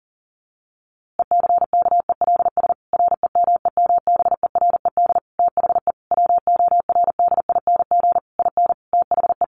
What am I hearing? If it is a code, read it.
Morse code, heard as a keyed tone, about 40 wpm, 700 Hz. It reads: ECKELS REMEMBERED THE WORDING IN THE
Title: A Sound Of Thunder 40wpm